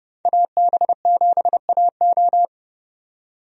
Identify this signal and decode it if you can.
Morse code, heard as a keyed tone, about 32 wpm, 700 Hz. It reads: AB7AO
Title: USA Call Signs - 5 Characters - 30wpm